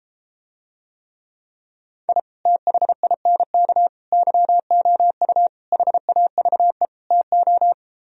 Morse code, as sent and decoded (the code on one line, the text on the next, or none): .. / - .... .. -. -.- / -.-- --- ..- / .... .- ...- . / - ---
I THINK YOU HAVE TO